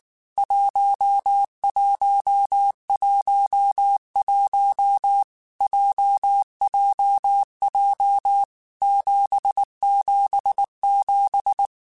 Morse code, heard as a keyed tone, about 19 wpm, 800 Hz. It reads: 1111 JJJ 777
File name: Basic Sets / Basic Letter Groups Collective7 / JZ17@18